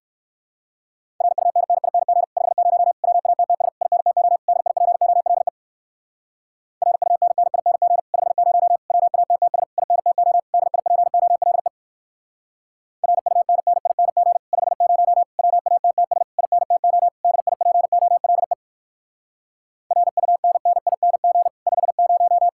Morse code, heard as a keyed tone, about 56 wpm, 700 Hz. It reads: RUNNING 50 WATTS INTO DIPOLE RUNNING 50 WATTS INTO DIPOLE RUNNING 50 WATTS INTO DIPOLE RUNNING 50